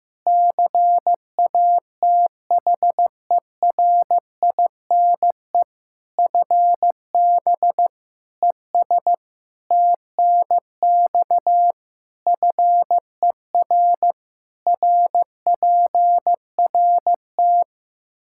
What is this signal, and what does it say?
Morse code, heard as a keyed tone, about 16 wpm, 700 Hz. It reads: CATHERINE FB ES TNX FER RPRT
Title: Practice QSO - Exchange 2 - Info - 15wpm